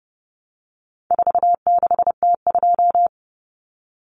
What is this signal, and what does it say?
Morse code, heard as a keyed tone, about 30 wpm, 700 Hz. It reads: 46T2
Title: Common Cut Numbers - 4-Digit 30wpm